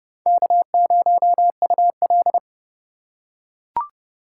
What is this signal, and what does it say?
Morse code, heard as a keyed tone, about 29 wpm, 700 Hz. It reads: K0UL